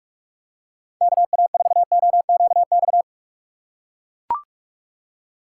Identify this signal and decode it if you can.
Morse code, heard as a keyed tone, about 43 wpm, 700 Hz. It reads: KA4OQX
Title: USA Call Signs - 6 Characters - 2x Character Spacing - 45wpm